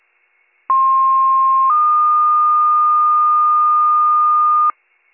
Two-tone page, heard as a one-second tone, then a three-second tone, about 1060 then 1220 Hz, with noise. It is heard from 0.7 to 4.7 s.